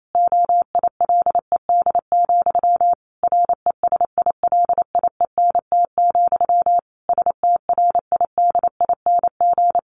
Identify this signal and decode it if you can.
Morse code, heard as a keyed tone, about 28 wpm, 700 Hz. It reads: OILED, RESILIENT, STRIDING
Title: A Sound Of Thunder 28wpm